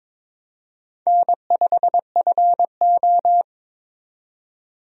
Morse code, heard as a keyed tone, about 22 wpm, 700 Hz. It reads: N5FO
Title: USA Call Signs - 4 Characters - 22wpm